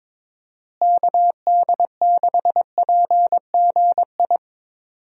Morse code, heard as a keyed tone, about 22 wpm, 700 Hz. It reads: KD6PGI
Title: USA Call Signs - 6 Characters - Repeated 3x - 22wpm